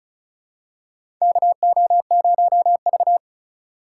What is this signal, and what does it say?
Morse code, heard as a keyed tone, about 36 wpm, 700 Hz. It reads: KO0V
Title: USA Call Signs - 4 Characters - 3x Character Spacing - 35wpm